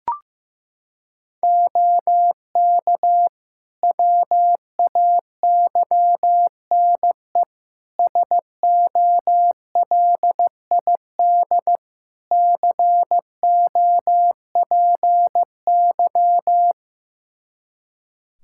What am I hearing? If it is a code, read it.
Morse code, heard as a keyed tone, about 15 wpm, 700 Hz. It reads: OK WAYNE SOLID COPY
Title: Practice QSO - Exchange 3 - More Info - 15wpm